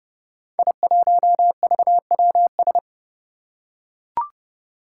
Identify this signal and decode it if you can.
Morse code, heard as a keyed tone, about 29 wpm, 700 Hz. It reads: I1VWS
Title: Five Challenging Letter-Number Combinations 30wpm